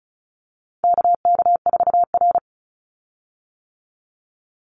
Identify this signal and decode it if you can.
Morse code, heard as a keyed tone, about 34 wpm, 700 Hz. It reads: KK4R
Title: Call Signs - 35wpm